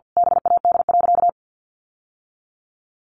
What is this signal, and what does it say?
Morse code, heard as a keyed tone, about 50 wpm, 700 Hz. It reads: BAD.
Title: Sentences from Top 1000 Words 50wpm